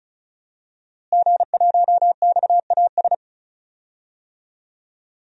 Morse code, heard as a keyed tone, about 36 wpm, 700 Hz. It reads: G1XAS
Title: DXCC-100 Call Signs - 5 Characters - Repeated 3x - 35wpm